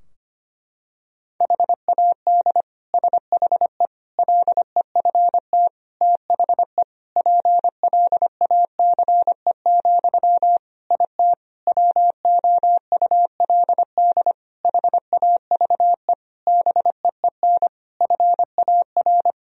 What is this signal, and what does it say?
Morse code, heard as a keyed tone, about 25 wpm, 700 Hz. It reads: HAD SHE LEFT THE PLACE, IT WOULD HAVE BEEN FAR
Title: Sentences from Top 1000 Words - Review Part 3 - 25wpm